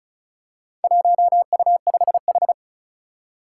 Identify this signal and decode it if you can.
Morse code, heard as a keyed tone, about 34 wpm, 700 Hz. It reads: 1U5H
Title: Four Challenging Letter-Number Combinations 35wpm